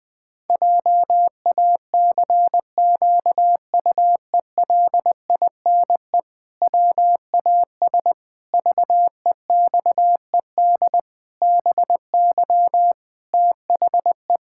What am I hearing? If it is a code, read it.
Morse code, heard as a keyed tone, about 20 wpm, 700 Hz. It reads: JACQUELINE WAS VEXED BY THE
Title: Pangram Challenge v2 - 20wpm